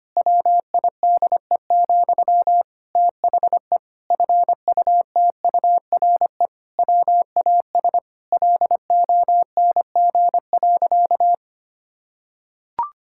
Morse code, heard as a keyed tone, about 25 wpm, 700 Hz. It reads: WIDE, THE FUTURE WAS LONG.
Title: Sentences from Top 1000 Words - Review Part 6 - 25wpm